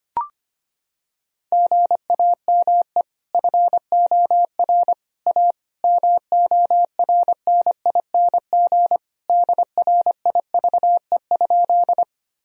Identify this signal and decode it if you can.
Morse code, heard as a keyed tone, about 25 wpm, 700 Hz. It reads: GAME FOR A MORNING DRIVE?
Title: Sentences from Top 1000 Words - Review Part 3 - 25wpm